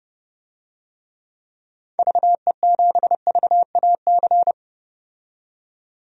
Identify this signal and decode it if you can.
Morse code, heard as a keyed tone, about 30 wpm, 700 Hz. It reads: VE7VAC
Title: DXCC-100 Call Signs - 6 Characters - 30wpm